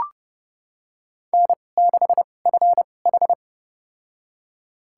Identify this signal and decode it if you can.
Morse code, heard as a keyed tone, about 30 wpm, spaced as for 22 wpm, 700 Hz. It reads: N6FH
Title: USA Call Signs - 4 Characters - 2x Character Spacing - 30wpm